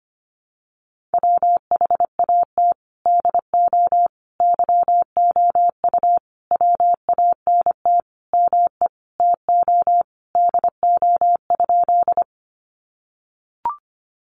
Morse code, heard as a keyed tone, about 25 wpm, 700 Hz. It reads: WHAT DO YOU WANT ME TO DO?
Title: Sentences from Top 200 Words 25wpm